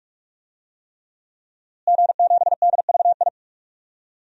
Morse code, heard as a keyed tone, about 41 wpm, 700 Hz. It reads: G7DVI
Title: DXCC-100 Call Signs - 5 Characters - Repeated 3x - 45wpm